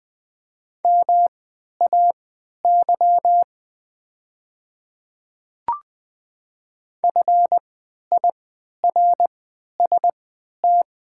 Morse code, heard as a keyed tone, about 20 wpm, spaced as for 11 wpm, 700 Hz. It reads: MAY FIRST
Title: 3x Character Spacing - Top 100 Words - 20wpm